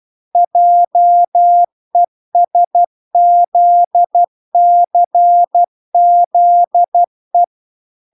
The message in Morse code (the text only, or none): JESZCZE